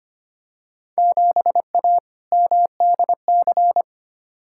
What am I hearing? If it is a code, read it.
Morse code, heard as a keyed tone, about 25 wpm, 700 Hz. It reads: 7A MDC
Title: Field Day Exchanges Encore - 25wpm